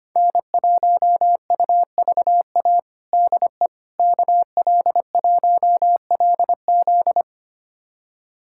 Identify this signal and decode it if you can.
Morse code, heard as a keyed tone, about 25 wpm, 700 Hz. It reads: N1UVA DE KL1LZ
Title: USA Callsign DE Callsign - 5 Characters - 25wpm